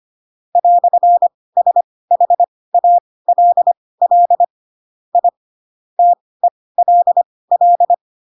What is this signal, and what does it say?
Morse code, heard as a keyed tone, about 25 wpm, spaced as for 18 wpm, 700 Hz. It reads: "SHALL I TELL
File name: Pinocchio / Pinocchio ch4 12 25 ik1hge